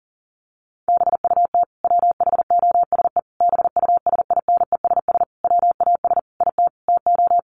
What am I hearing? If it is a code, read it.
Morse code, heard as a keyed tone, about 40 wpm, 700 Hz. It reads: BUT WHOSE BUSINESS WAS IT TO
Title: Sentences from Top 1500 Words 40wpm